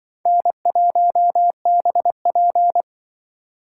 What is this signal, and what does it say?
Morse code, heard as a keyed tone, about 24 wpm, 700 Hz. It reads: N1BP
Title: Speed Racing - USA 1x2 Call Signs 20wpm